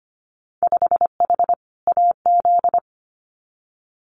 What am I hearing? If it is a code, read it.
Morse code, heard as a keyed tone, about 25 wpm, 700 Hz. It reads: 5H AZ
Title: Winter Field Day Exchanges Encore 25wpm